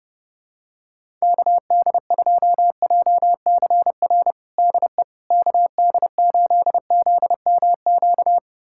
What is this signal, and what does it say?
Morse code, heard as a keyed tone, about 30 wpm, 700 Hz. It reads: KD2JCR DE KD8ZMQ
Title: USA Callsign DE Callsign 30wpm